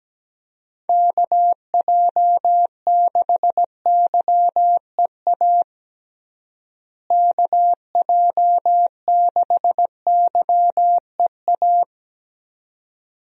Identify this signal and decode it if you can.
Morse code, heard as a keyed tone, about 17 wpm, 700 Hz. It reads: KJ6YEA KJ6YEA
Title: USA Call Signs - 6 Characters - Repeated 2x - 17wpm